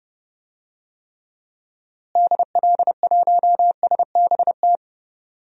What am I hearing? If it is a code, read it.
Morse code, heard as a keyed tone, about 31 wpm, 700 Hz. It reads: DL1SBT
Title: DXCC-100 Call Signs - 6 Characters - Repeated 3x - 30wpm